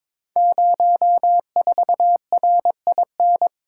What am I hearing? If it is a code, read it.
Morse code, heard as a keyed tone, about 22 wpm, 700 Hz. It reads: 04RIN
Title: Code Groups - 5 Letter:Numbers - 22wpm